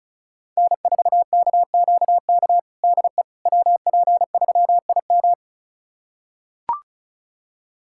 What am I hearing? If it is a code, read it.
Morse code, heard as a keyed tone, about 34 wpm, 700 Hz. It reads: N4KQK DE WP3IM
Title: USA Callsign DE Callsign - 5 Characters - 35wpm